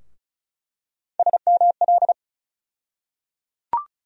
Morse code, heard as a keyed tone, about 34 wpm, 700 Hz. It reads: SML
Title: USA Call Sign Postfixes - 3 Letters 35wpm